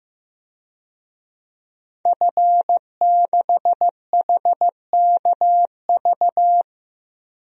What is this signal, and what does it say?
Morse code, heard as a keyed tone, about 15 wpm, 700 Hz. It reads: F6HKV
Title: DXCC-100 Call Signs - 5 Characters - 15wpm